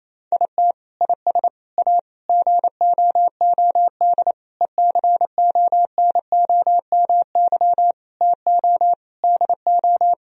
Morse code, heard as a keyed tone, about 28 wpm, 700 Hz. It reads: IT IS A GOOD ECONOMY TO DO
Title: Sentences from Top 1000 Words - Review Part 3 - 28wpm